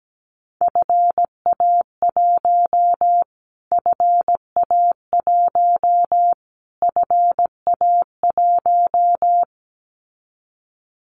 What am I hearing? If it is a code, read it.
Morse code, heard as a keyed tone, about 17 wpm, 700 Hz. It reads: FA1 FA1 FA1
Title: Callsign Prefixes for Top 25 DXCC - Repeated 3x - 17wpm